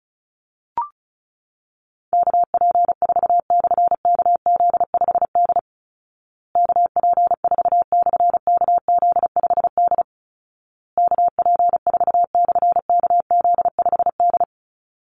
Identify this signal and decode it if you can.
Morse code, heard as a keyed tone, about 34 wpm, 700 Hz. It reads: KP4/KZ5D KP4/KZ5D KP4/KZ5D
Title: Call Signs Repeated 2x - 35wpm